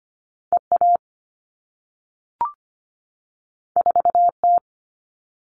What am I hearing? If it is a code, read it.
Morse code, heard as a keyed tone, about 25 wpm, 700 Hz. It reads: EA 4T
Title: All Cut Numbers - 2-Digit 25wpm